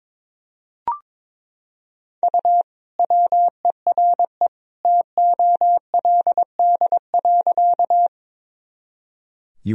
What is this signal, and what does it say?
Morse code, heard as a keyed tone, about 22 wpm, 700 Hz. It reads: U WERE TOLD.
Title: Sentences from Top 400 Words and Abbreviations 22wpm